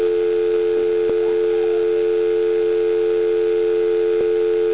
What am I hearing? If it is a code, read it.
Dial tone (North America), 350 + 440 Hz.
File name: dialtone.wav / dialtone